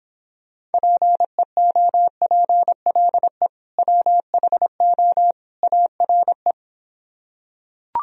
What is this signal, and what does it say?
Morse code, heard as a keyed tone, about 26 wpm, 700 Hz. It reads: PEOPLE WHO ARE